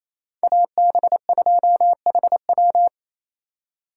Morse code, heard as a keyed tone, about 36 wpm, 700 Hz. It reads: AB2HW